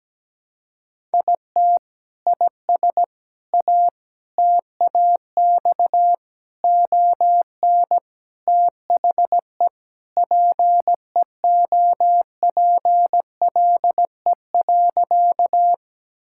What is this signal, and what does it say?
Morse code, heard as a keyed tone, about 17 wpm, 700 Hz. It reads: IT IS A TAX ON THE PEOPLE.
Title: Sentences from Top 1000 Words - Review Part 5 - 17wpm